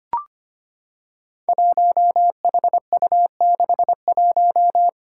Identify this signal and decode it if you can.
Morse code, heard as a keyed tone, about 25 wpm, 700 Hz. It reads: 1HU61
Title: Five Challenging Letter-Number Combinations 25wpm